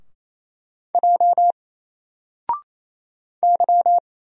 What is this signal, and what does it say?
Morse code, heard as a keyed tone, about 27 wpm, 700 Hz. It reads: J Y
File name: Single-Letter-Number-28wpm